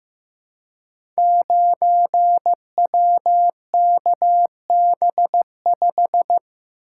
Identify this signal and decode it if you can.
Morse code, heard as a keyed tone, about 15 wpm, 700 Hz. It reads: 9WKB5
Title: Code Groups - 5 Letter:Numbers - 15wpm